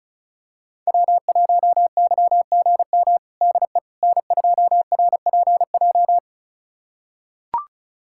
Morse code, heard as a keyed tone, about 36 wpm, 700 Hz. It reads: W1YGM DE N2RPJ
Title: USA Callsign DE Callsign - 5 Characters - 35wpm